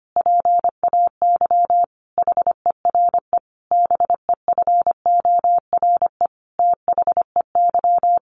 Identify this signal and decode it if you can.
Morse code, heard as a keyed tone, about 25 wpm, 700 Hz. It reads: PAY HERE BEFORE THEY
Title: Sentences from Top 700 Words 25wpm